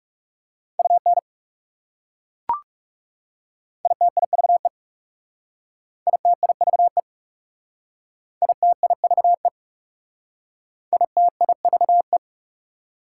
Morse code, heard as a keyed tone, about 37 wpm, 700 Hz. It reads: UN ITIVE ITIVE ITIVE ITIVE
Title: Speed-Racing Prefixes and Suffixes 30wpm